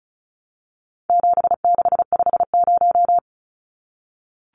Morse code, heard as a keyed tone, about 34 wpm, 700 Hz. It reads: 7650
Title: Numbers - 4-Digit 35wpm